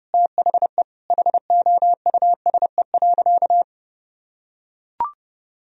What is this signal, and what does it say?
Morse code, heard as a keyed tone, about 30 wpm, 700 Hz. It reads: THE HOUSE.
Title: Sentences from Top 1000 Words - Review Part 4 - 30wpm